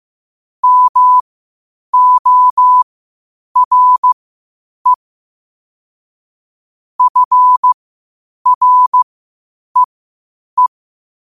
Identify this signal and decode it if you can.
Morse code, heard as a keyed tone, about 15 wpm, spaced as for 8 wpm, 1000 Hz. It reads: MORE FREE
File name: qotdfiles / 8wpm